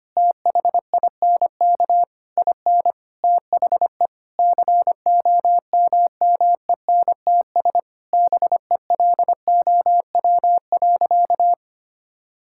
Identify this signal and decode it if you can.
Morse code, heard as a keyed tone, about 25 wpm, 700 Hz. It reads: THINK IN THE COMMENTS BELOW.
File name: practice / 25wpm / Sentences with 9-Word Phrases - 25wpm